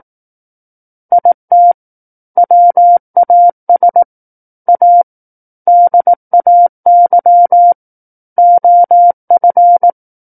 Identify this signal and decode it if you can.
Morse code, heard as a keyed tone, about 18 wpm, 700 Hz. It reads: IT WAS A DAY OF
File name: 18 WPM